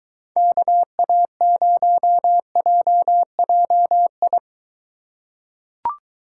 Morse code, heard as a keyed tone, about 22 wpm, 700 Hz. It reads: KA0JJI